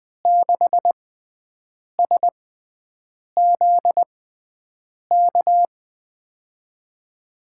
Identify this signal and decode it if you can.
Morse code, heard as a keyed tone, about 20 wpm, spaced as for 7 wpm, 700 Hz. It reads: 6SZK
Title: USA Call Signs - 6x Character Spacing - 20wpm